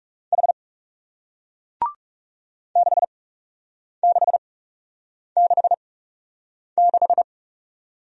Morse code, heard as a keyed tone, about 42 wpm, spaced as for 8 wpm, 700 Hz. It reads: H 6666